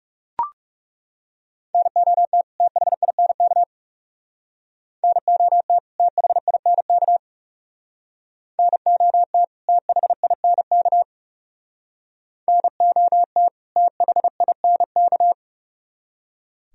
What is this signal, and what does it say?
Morse code, heard as a keyed tone, about 39 wpm, 700 Hz. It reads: NOT THINK NOT THINK NOT THINK NOT THINK